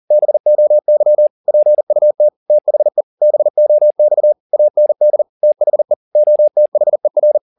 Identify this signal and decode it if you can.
Morse code, heard as a keyed tone, about 40 wpm, 600 Hz. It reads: BOY PUT THE BOX AND THE OTHER